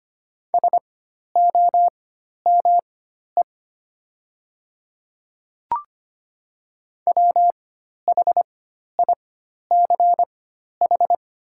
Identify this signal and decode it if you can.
Morse code, heard as a keyed tone, about 25 wpm, spaced as for 12 wpm, 700 Hz. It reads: SOME WHICH
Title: 4x Character Spacing - Top 500 Words - 25wpm